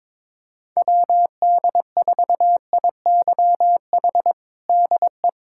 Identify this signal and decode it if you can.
Morse code, heard as a keyed tone, about 22 wpm, 700 Hz. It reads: WD4IYH DE
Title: Practice QSO - Exchange 4 - Ending - 22wpm